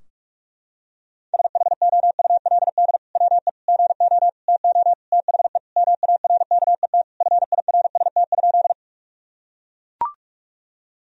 Morse code, heard as a keyed tone, about 42 wpm, 700 Hz. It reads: SHOULD WE GO TO THE MARKET FIRST?
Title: Sentences from Top 1000 Words - Review Part 3 - 45wpm